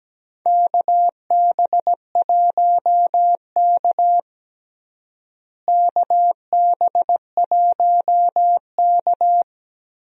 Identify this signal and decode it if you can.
Morse code, heard as a keyed tone, about 17 wpm, 700 Hz. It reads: KB1K KB1K
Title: USA Call Signs - 4 Characters - Repeated 2x - 17wpm